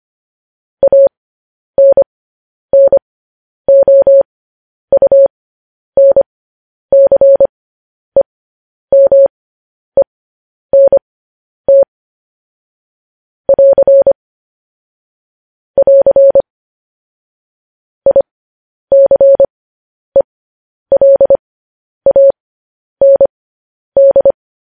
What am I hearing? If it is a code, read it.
Morse code, heard as a keyed tone, about 25 wpm, spaced as for 10 wpm, 550 Hz. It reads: ANNOUNCEMENT <AR> <AR> ICELAND
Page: news headlines in morse code at 10 wpm